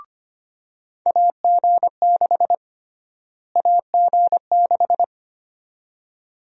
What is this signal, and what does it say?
Morse code, heard as a keyed tone, about 25 wpm, 700 Hz. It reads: AG6 AG6
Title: USA Call Sign Prefixes - Repeated 2x - 25wpm